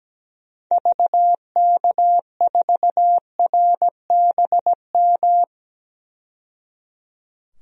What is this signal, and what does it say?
Morse code, heard as a keyed tone, about 17 wpm, 700 Hz. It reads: VK4RBM
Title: DXCC-100 Call Signs - 6 Characters - 17wpm